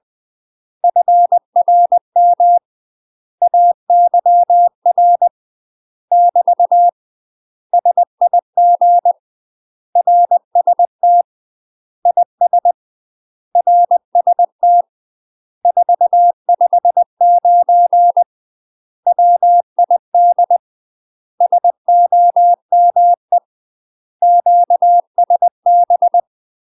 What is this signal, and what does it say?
Morse code, heard as a keyed tone, about 20 wpm, 700 Hz. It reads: FRM AYR <BT> SIG RST IS RST 459 WID SOME QSB
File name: UK MASTER QSOs TRAINING/20WPM